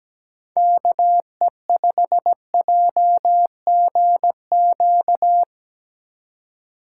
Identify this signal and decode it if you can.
Morse code, heard as a keyed tone, about 18 wpm, 700 Hz. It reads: KE5JGQ
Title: Speed Racing - USA 2x3 Call Signs 17wpm